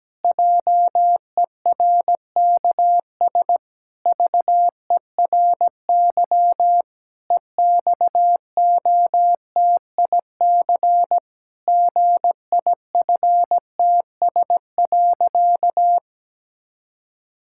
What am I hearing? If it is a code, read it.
Morse code, heard as a keyed tone, about 17 wpm, 700 Hz. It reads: JERKS VERY EXOTIC GIFTS.
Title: Pangram Challenge v2 - 17wpm